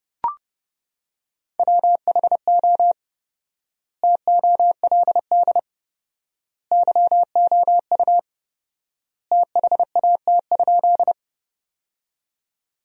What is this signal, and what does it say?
Morse code, heard as a keyed tone, about 31 wpm, 700 Hz. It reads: WHO TOLD YOU THAT?
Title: 4x Word-Spacing - Sentences from Top 400 Words - 30wpm